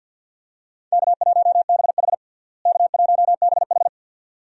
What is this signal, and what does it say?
Morse code, heard as a keyed tone, about 52 wpm, 700 Hz. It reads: K1BH K1BH
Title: USA Call Signs - 4 Characters - Repeated 2x - 50wpm